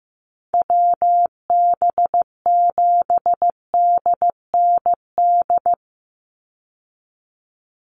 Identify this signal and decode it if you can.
Morse code, heard as a keyed tone, about 14 wpm, 700 Hz. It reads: WB7DND
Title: Call Signs Repeated 2x - 15wpm